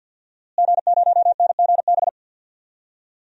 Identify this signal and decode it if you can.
Morse code, heard as a keyed tone, about 46 wpm, 700 Hz. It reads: G0NGB
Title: DXCC-100 Call Signs - 5 Characters - Repeated 3x - 50wpm